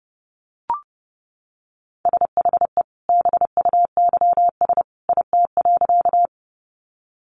Morse code, heard as a keyed tone, about 30 wpm, 700 Hz. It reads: SHE BUYS IT.
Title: Additional Easy Phrases 30wpm